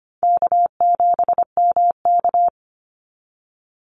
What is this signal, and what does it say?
Morse code, heard as a keyed tone, about 25 wpm, 700 Hz. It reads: K7MK
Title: NA SOTA Activator Callsigns 25wpm